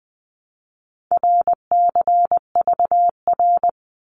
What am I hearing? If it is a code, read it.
Morse code, heard as a keyed tone, about 20 wpm, 700 Hz. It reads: RCVR
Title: Abbreviations 20wpm